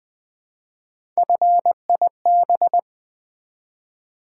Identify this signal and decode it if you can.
Morse code, heard as a keyed tone, about 20 wpm, 700 Hz. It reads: FIB
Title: Build A Word - 1 Letter At A Time - 5-Letter Words - 20wpm